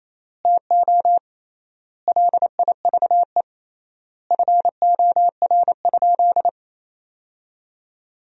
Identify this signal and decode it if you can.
Morse code, heard as a keyed tone, about 28 wpm, 700 Hz. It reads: TO LIVE FOR?
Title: 3x Word-Spacing - Sentences from Top 200 Words - 28wpm